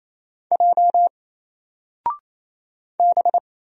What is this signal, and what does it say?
Morse code, heard as a keyed tone, about 27 wpm, 700 Hz. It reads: J B
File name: Single-Letter-Number-28wpm